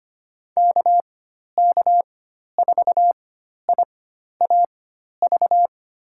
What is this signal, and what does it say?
Morse code, heard as a keyed tone, about 25 wpm, spaced as for 12 wpm, 700 Hz. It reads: KK4IAV
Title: USA Call Signs - 6 Characters - 4x Character Spacing - 25wpm